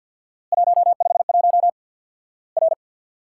Morse code, heard as a keyed tone, about 50 wpm, 700 Hz. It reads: 151 R
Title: Worked All Europe DX Contest - QTC Exchange with Repeats - 50wpm